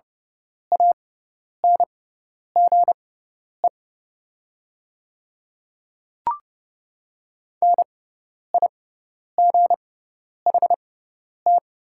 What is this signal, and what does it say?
Morse code, heard as a keyed tone, about 30 wpm, spaced as for 10 wpm, 700 Hz. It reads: ANGE NIGHT
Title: 6x Character Spacing - Top 300 Words - 30wpm